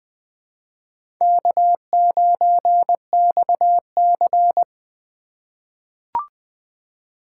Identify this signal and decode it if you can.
Morse code, heard as a keyed tone, about 20 wpm, 700 Hz. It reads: K9XC
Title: USA Call Signs - 4x Character Spacing - 20wpm